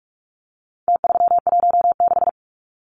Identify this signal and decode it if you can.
Morse code, heard as a keyed tone, about 44 wpm, 700 Hz. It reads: T316
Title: Numbers - Contest Style - 4-Digit 45wpm